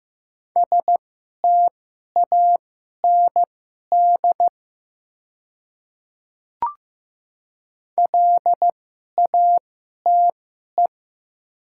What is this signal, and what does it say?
Morse code, heard as a keyed tone, about 15 wpm, spaced as for 11 wpm, 700 Hz. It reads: STAND LATE
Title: 2x Character Spacing - Top 300 Words - 15wpm